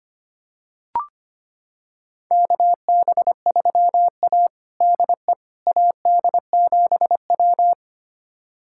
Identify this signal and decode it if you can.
Morse code, heard as a keyed tone, about 25 wpm, 700 Hz. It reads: KB3A DE AD7W
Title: USA Callsign DE Callsign - 4 Characters - 25wpm